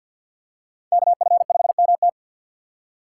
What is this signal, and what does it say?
Morse code, heard as a keyed tone, about 50 wpm, 700 Hz. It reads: KF5MT
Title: USA Call Signs - 5 Characters - 6x Character Spacing - 50wpm